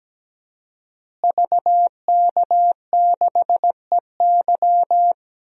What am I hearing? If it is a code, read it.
Morse code, heard as a keyed tone, about 17 wpm, 700 Hz. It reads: VK6EY
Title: DXCC-100 Call Signs - 5 Characters - Repeated 2x - 17wpm